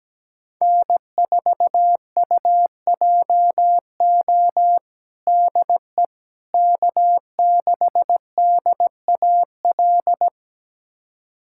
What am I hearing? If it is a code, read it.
Morse code, heard as a keyed tone, about 17 wpm, 700 Hz. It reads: N4UJO DE K6DAL
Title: USA Callsign DE Callsign - 5 Characters - 17wpm